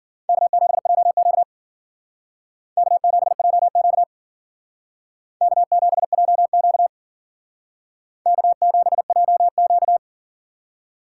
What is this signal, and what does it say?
Morse code, heard as a keyed tone, about 51 wpm, 700 Hz. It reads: K7JQ K7JQ K7JQ K7JQ